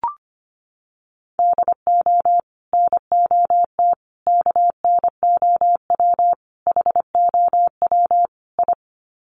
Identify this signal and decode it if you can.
Morse code, heard as a keyed tone, about 25 wpm, 700 Hz. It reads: DO NOT KNOW HOW I
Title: Sets of 5 Words Encore2 25wpm